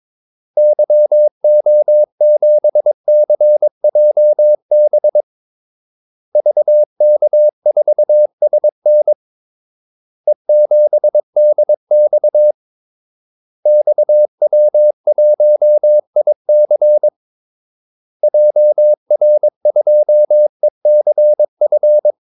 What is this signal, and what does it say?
Morse code, heard as a keyed tone, about 22 wpm, 600 Hz. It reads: YO7CJB VK4SN E7DX XW1IC JR2ECF
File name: Callsigns-22wpm